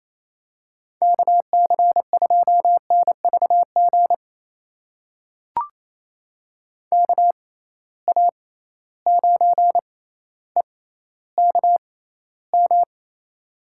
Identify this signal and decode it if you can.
Morse code, heard as a keyed tone, about 28 wpm, spaced as for 14 wpm, 700 Hz. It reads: KC2NVG KA9EKM
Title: USA Call Signs - 6 Characters - 6x Character Spacing - 28wpm